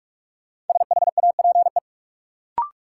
Morse code, heard as a keyed tone, about 41 wpm, 700 Hz. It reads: SHAPE
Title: Sentences from Top 500 Words - New Word Drill - 45wpm